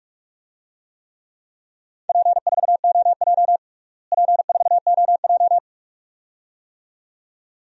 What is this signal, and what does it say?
Morse code, heard as a keyed tone, about 46 wpm, 700 Hz. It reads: P4OJ P4OJ
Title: DXCC-100 Call Signs - 4 Characters - Repeated 2x - 45wpm